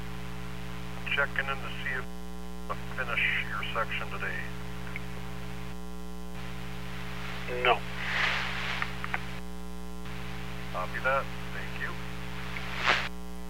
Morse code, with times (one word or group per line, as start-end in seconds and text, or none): none